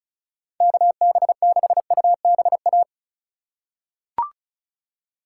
Morse code, heard as a keyed tone, about 34 wpm, 700 Hz. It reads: KB6UBA
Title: USA Call Signs - 6 Characters - 2x Character Spacing - 35wpm